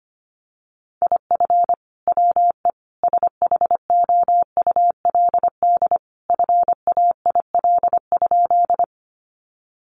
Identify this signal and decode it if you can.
Morse code, heard as a keyed tone, about 25 wpm, 700 Hz. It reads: IF WE SHOULD FAIL?
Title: Sentences from Top 1100 Words 25wpm